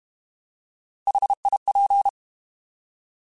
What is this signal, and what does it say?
Morse code, heard as a keyed tone, about 32 wpm, 800 Hz. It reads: HIP